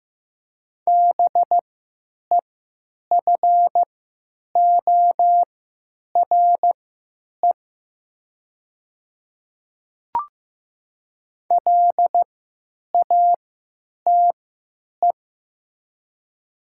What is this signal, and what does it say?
Morse code, heard as a keyed tone, about 15 wpm, spaced as for 8 wpm, 700 Hz. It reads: BEFORE LATE
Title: 3x Character Spacing - Top 300 Words - 15wpm